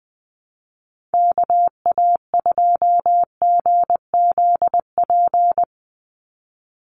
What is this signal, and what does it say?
Morse code, heard as a keyed tone, about 20 wpm, 700 Hz. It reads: KA2GZP
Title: USA Call Signs 20wpm